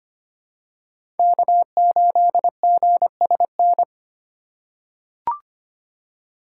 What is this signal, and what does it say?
Morse code, heard as a keyed tone, about 25 wpm, 700 Hz. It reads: K8GSN
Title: USA Call Signs - 5 Characters - 3x Character Spacing - 25wpm